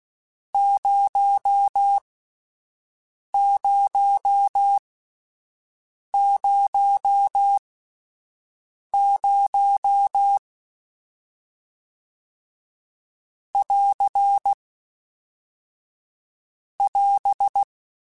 Morse code, heard as a keyed tone, about 16 wpm, spaced as for 5 wpm, 800 Hz. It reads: 0000 <AR> <AS>